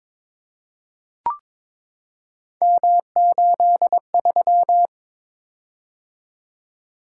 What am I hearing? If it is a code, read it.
Morse code, heard as a keyed tone, about 23 wpm, 700 Hz. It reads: M83
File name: practice / 22wpm / Three Letter-Number Combinations - 22wpm